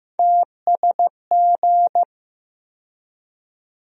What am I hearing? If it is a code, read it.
Morse code, heard as a keyed tone, about 16 wpm, 700 Hz. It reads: TSG
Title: USA Call Sign Postfixes - 3 Letters 15wpm